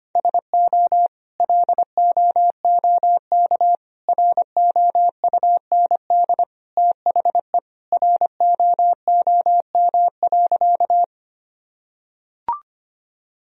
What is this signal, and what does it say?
Morse code, heard as a keyed tone, about 25 wpm, 700 Hz. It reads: SO LOOK ROUND THE ROOM.
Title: Sentences from Top 1000 Words - Review Part 1 - 25wpm